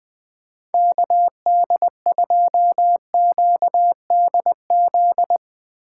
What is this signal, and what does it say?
Morse code, heard as a keyed tone, about 22 wpm, 700 Hz. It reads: KD2QDZ
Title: Speed Racing - USA 2x3 Call Signs 20wpm